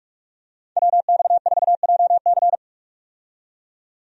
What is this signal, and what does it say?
Morse code, heard as a keyed tone, about 43 wpm, 700 Hz. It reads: WX4JC
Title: USA Call Signs - 5 Characters - 45wpm